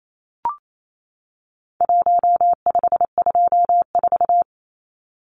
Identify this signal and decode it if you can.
Morse code, heard as a keyed tone, about 28 wpm, 700 Hz. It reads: 1524
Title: Numbers - 4-Digit 28wpm